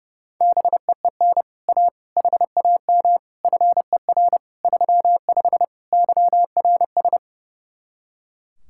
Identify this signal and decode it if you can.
Morse code, heard as a keyed tone, about 36 wpm, 700 Hz. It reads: BEEN A HAM FER 35 YRS
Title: Speed-Racing Common QSO Phrases - 30wpm